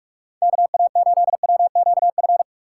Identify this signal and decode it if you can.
Morse code, heard as a keyed tone, about 45 wpm, 700 Hz. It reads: KA8WQF
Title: USA Call Signs - 6 Characters - 3x Character Spacing - 45wpm